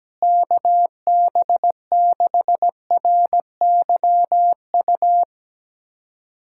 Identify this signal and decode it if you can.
Morse code, heard as a keyed tone, about 17 wpm, 700 Hz. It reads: KB6RYU
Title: USA Callsign DE Callsign - 6 Characters - 17wpm